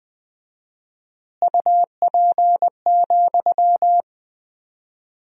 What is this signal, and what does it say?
Morse code, heard as a keyed tone, about 21 wpm, 700 Hz. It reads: UP,
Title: 4x Word-Spacing - Sentences from Top 500 Words - 20wpm